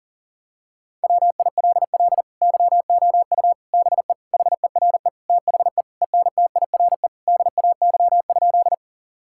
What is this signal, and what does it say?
Morse code, heard as a keyed tone, about 40 wpm, 700 Hz. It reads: WILL YOU BE HERE THE ENTIRE DAY?
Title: Sentences from Top 1000 Words - Review Part 5 - 40wpm